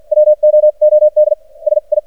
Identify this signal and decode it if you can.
Morse code, heard as a keyed tone, about 52 wpm, spaced as for 42 wpm, 610 Hz. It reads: WOOD SI
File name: archived / CRSnet